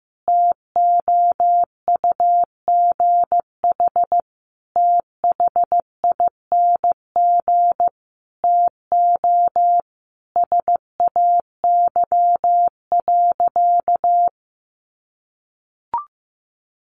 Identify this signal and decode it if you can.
Morse code, heard as a keyed tone, about 15 wpm, 700 Hz. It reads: TOUGH THING TO SAY.
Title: Sentences from Top 1200 Words 15wpm